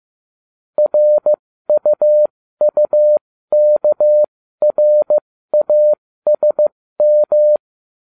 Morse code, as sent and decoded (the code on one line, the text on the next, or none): .-. ..- ..- -.- .-. .- ... --
RUUKRASM